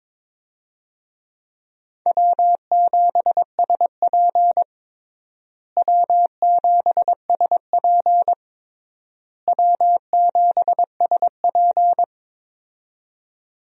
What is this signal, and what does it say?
Morse code, heard as a keyed tone, about 22 wpm, 700 Hz. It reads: W7SP W7SP W7SP
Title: USA Call Signs - 4 Characters - Repeated 3x - 22wpm